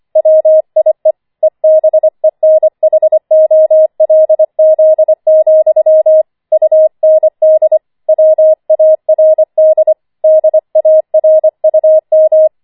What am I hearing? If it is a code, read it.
Morse code, heard as a keyed tone, about 25 wpm, 630 Hz. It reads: WIE EBENHOLZ, UND WARD DARUM
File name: 25wpm